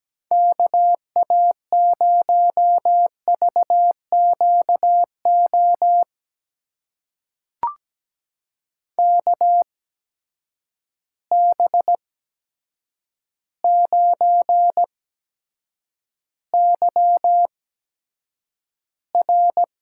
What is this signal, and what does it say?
Morse code, heard as a keyed tone, about 17 wpm, spaced as for 8 wpm, 700 Hz. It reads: KA0VQO KB9YR
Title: USA Call Signs - 6 Characters - 8x Character Spacing - 17wpm